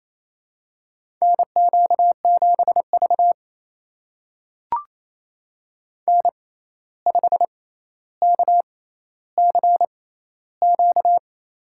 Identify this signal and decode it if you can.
Morse code, heard as a keyed tone, about 29 wpm, spaced as for 14 wpm, 700 Hz. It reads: NQ7V N5KCQ
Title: USA Call Signs - 6x Character Spacing - 28wpm